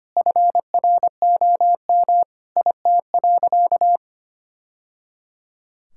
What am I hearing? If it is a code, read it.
Morse code, heard as a keyed tone, about 25 wpm, 700 Hz. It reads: FROM IT.
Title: Sentences from Top 1000 Words - Review Part 5 - 25wpm